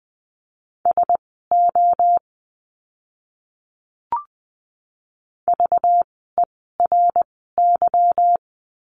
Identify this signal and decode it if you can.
Morse code, heard as a keyed tone, about 20 wpm, spaced as for 14 wpm, 700 Hz. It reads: SO VERY
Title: 230 Lesson 37f - Letter x - Words 20wpm